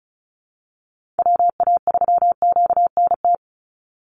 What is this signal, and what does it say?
Morse code, heard as a keyed tone, about 34 wpm, 700 Hz. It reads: WA3QNT
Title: Call Signs Repeated 2x - 35wpm